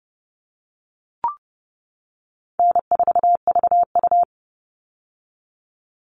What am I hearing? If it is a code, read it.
Morse code, heard as a keyed tone, about 30 wpm, 700 Hz. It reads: N4VU
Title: All Cut Numbers - 4-Digit 30wpm